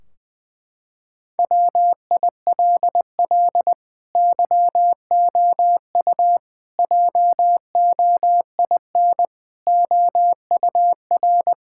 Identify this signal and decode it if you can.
Morse code, heard as a keyed tone, about 20 wpm, 700 Hz. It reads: WILL YOU JOIN OUR
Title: Sentences from Top 1600 Words 20wpm